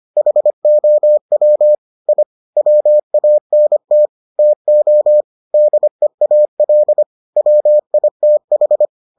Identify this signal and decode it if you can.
Morse code, heard as a keyed tone, about 25 wpm, 600 Hz. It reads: HOW I WANT TO DEAL WITH